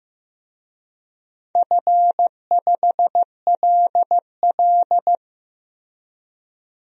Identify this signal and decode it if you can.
Morse code, heard as a keyed tone, about 15 wpm, 700 Hz. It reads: F5LL
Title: DXCC-100 Call Signs - 4 Characters - Repeated 2x - 15wpm